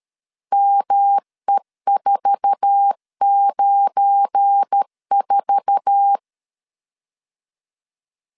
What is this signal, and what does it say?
Morse code, heard as a keyed tone, about 13 wpm, 800 Hz. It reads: ME494
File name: LEZIONI / 8 YJ45 / 8 YJ4 12wpm 2151